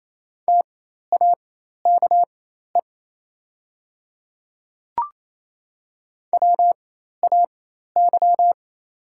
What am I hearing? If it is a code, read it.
Morse code, heard as a keyed tone, about 28 wpm, spaced as for 13 wpm, 700 Hz. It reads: TAKE WAY
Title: 4x Character Spacing - Top 200 Words - 28wpm